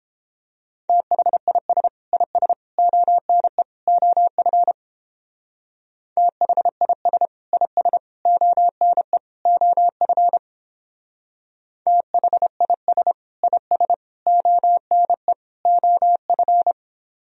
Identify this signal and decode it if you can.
Morse code, heard as a keyed tone, about 31 wpm, 700 Hz. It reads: THIS IS ONE OF THIS IS ONE OF THIS IS ONE OF